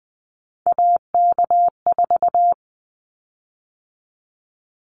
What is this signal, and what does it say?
Morse code, heard as a keyed tone, about 19 wpm, 700 Hz. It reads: AK4
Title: Callsign Prefixes for Top 25 DXCC - 20wpm